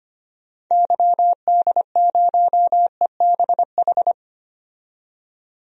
Morse code, heard as a keyed tone, about 25 wpm, 700 Hz. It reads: YD0EBH
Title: DXCC-100 Call Signs - 6 Characters - 25wpm